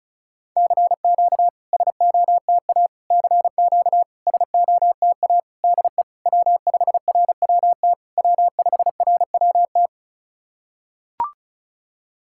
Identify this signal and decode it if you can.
Morse code, heard as a keyed tone, about 36 wpm, 700 Hz. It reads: CQ SOTA CQ SOTA DE W5RWT W5RWT
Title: SOTA QSO Practice 35wpm